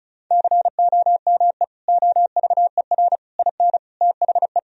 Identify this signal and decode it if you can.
Morse code, heard as a keyed tone, about 36 wpm, 700 Hz. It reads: COME OVER IN THE
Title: Sentences from Top 1000 Words - Review Part 1 - 35wpm